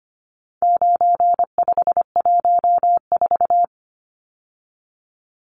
Morse code, heard as a keyed tone, about 25 wpm, 700 Hz. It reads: 9514
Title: Numbers - 4-Digit 25wpm